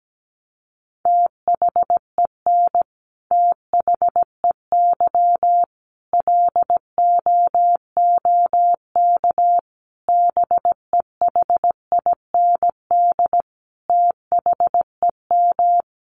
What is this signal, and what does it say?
Morse code, heard as a keyed tone, about 17 wpm, 700 Hz. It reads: THEN THEY LOOK BEHIND THEM
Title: Sentences from Top 1300 Words 17wpm